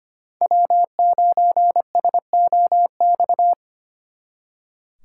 Morse code, heard as a keyed tone, about 28 wpm, 700 Hz. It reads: W9SOX
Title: Speed Racing - USA 1x3 Call Signs 25wpm